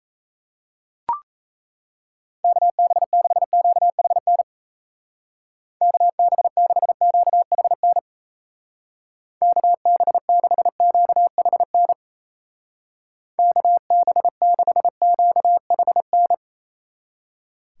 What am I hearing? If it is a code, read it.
Morse code, heard as a keyed tone, about 34 wpm, 700 Hz. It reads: KB6QHN KB6QHN KB6QHN KB6QHN
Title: Speed Racing - USA 2x3 Call Signs 28wpm